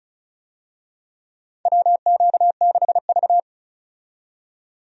Morse code, heard as a keyed tone, about 34 wpm, 700 Hz. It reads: WQ6V
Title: USA Call Signs - 4 Characters - 35wpm